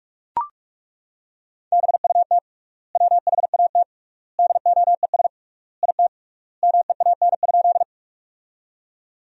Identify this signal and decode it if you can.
Morse code, heard as a keyed tone, about 41 wpm, 700 Hz. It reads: BUT WHAT DOES IT MEAN?
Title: 3x Word-Spacing - Sentences from Top 200 Words - 45wpm